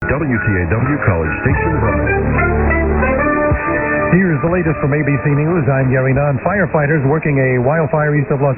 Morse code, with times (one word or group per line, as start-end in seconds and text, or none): none